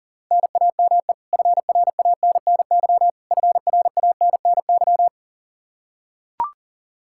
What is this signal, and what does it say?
Morse code, heard as a keyed tone, about 40 wpm, 700 Hz. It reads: NAME FRANNY FRANNY
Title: Practice QSO - Exchange 1 - Beginning - 40wpm